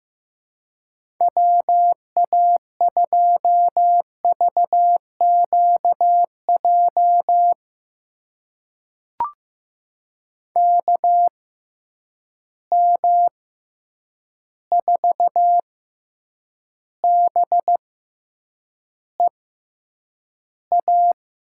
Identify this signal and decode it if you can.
Morse code, heard as a keyed tone, about 15 wpm, spaced as for 8 wpm, 700 Hz. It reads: WA2VQJ KM4BEA
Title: USA Call Signs - 6 Characters - 6x Character Spacing - 15wpm